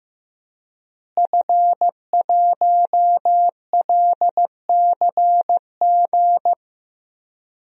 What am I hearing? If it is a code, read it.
Morse code, heard as a keyed tone, about 15 wpm, 700 Hz. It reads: F1LCG
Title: DXCC-100 Call Signs - 5 Characters - Repeated 3x - 15wpm